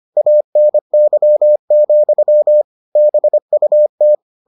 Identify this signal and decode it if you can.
Morse code, heard as a keyed tone, about 25 wpm, 600 Hz. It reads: ANY, BUT